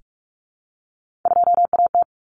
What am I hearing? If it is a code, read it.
Morse code, heard as a keyed tone, about 44 wpm, 700 Hz. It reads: 2AT
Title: Common Cut Numbers - 3-Digit 45wpm